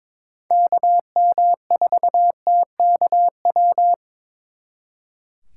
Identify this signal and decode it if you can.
Morse code, heard as a keyed tone, about 25 wpm, 700 Hz. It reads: KM4TKW
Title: Speed Racing - USA 2x3 Call Signs 22wpm